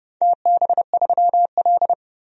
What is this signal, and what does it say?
Morse code, heard as a keyed tone, about 30 wpm, 700 Hz. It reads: TB3L
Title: DXCC-100 Call Signs - 4 Characters - 30wpm